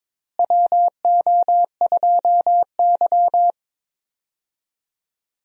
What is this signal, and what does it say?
Morse code, heard as a keyed tone, about 21 wpm, 700 Hz. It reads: WO2Y
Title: USA Call Signs - 4 Characters - Repeated 3x - 22wpm